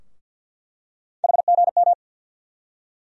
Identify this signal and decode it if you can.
Morse code, heard as a keyed tone, about 53 wpm, 700 Hz. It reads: HGM
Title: USA Call Sign Postfixes - 3 Letters 50wpm